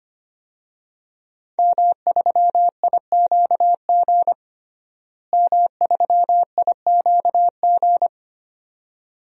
Morse code, heard as a keyed tone, about 25 wpm, 700 Hz. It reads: M3IQG M3IQG
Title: DXCC-100 Call Signs - 5 Characters - Repeated 2x - 25wpm